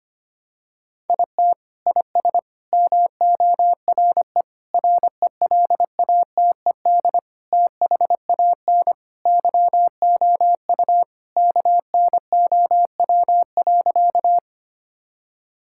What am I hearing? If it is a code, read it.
Morse code, heard as a keyed tone, about 25 wpm, 700 Hz. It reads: IT IS MORE RELATED THAN YOU KNOW.
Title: Sentences from Top 1000 Words - Review Part 5 - 25wpm